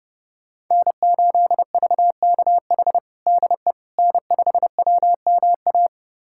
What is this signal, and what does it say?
Morse code, heard as a keyed tone, about 32 wpm, 700 Hz. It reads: N8VKH DE N5WMA
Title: USA Callsign DE Callsign - 5 Characters - 30wpm